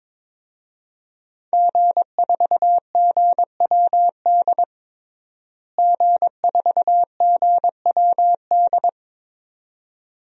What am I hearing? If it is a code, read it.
Morse code, heard as a keyed tone, about 22 wpm, 700 Hz. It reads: G4GWD G4GWD
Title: DXCC-100 Call Signs - 5 Characters - Repeated 2x - 22wpm